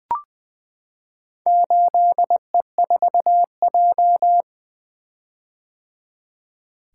Morse code, heard as a keyed tone, about 20 wpm, 700 Hz. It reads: 8E4J
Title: Four Letter-Number Combinations - 20wpm